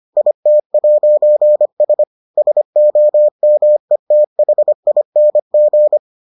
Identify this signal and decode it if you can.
Morse code, heard as a keyed tone, about 25 wpm, 600 Hz. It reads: IT'S SOMETHING